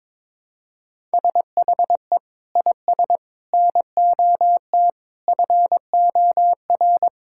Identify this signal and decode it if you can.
Morse code, heard as a keyed tone, about 22 wpm, 700 Hz. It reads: SHE IS NOT FOR
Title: Sentences from Top 1000 Words - Review Part 5 - 22wpm